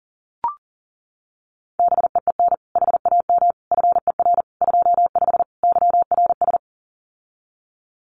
Morse code, heard as a keyed tone, about 40 wpm, 700 Hz. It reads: BEEN HAM FER 25 YRS
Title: Common QSO Phrases Encore 40wpm